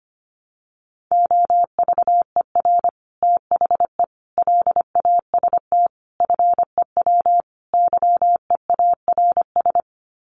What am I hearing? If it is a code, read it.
Morse code, heard as a keyed tone, about 25 wpm, 700 Hz. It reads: OVER THE LAST FEW YEARS
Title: Sets of 5 Words Encore 25wpm